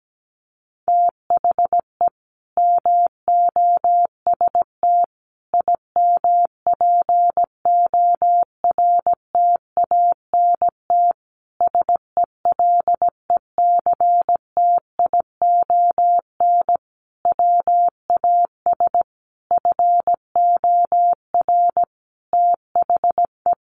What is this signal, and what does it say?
Morse code, heard as a keyed tone, about 17 wpm, 700 Hz. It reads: THE MOST IMPORTANT SELECTION WAS FOR THE
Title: Sentences from Top 1400 Words 17wpm